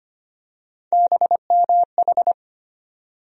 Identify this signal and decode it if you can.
Morse code, heard as a keyed tone, about 25 wpm, 700 Hz. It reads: BMH
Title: USA Call Sign Postfixes - 3 Letters 25wpm